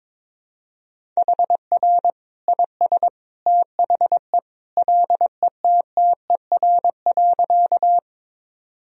Morse code, heard as a keyed tone, about 22 wpm, 700 Hz. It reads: HR IS THE LETTER.
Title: Sentences from Top 300 Words and Abbreviations 22wpm